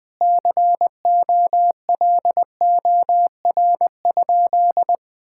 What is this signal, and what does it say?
Morse code, heard as a keyed tone, about 21 wpm, 700 Hz. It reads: COLOR?
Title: Sentences from Top 1000 Words - Review Part 6 - 20wpm